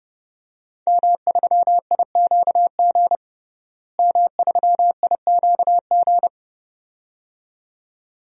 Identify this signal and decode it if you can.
Morse code, heard as a keyed tone, about 30 wpm, 700 Hz. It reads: M3IQG M3IQG
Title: DXCC-100 Call Signs - 5 Characters - Repeated 2x - 30wpm